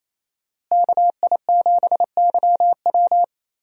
Morse code, heard as a keyed tone, about 28 wpm, 700 Hz. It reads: KI7YW
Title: USA Call Signs - 5 Characters - Repeated 3x - 28wpm